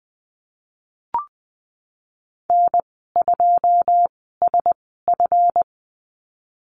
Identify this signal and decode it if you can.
Morse code, heard as a keyed tone, about 19 wpm, spaced as for 14 wpm, 700 Hz. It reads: N2SF
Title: 186 Lesson 30g - Letter b - Call Signs 20wpm